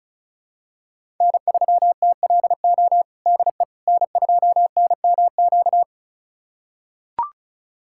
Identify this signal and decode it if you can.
Morse code, heard as a keyed tone, about 34 wpm, 700 Hz. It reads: N3TLO DE N2NMQ
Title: USA Callsign DE Callsign - 5 Characters - 35wpm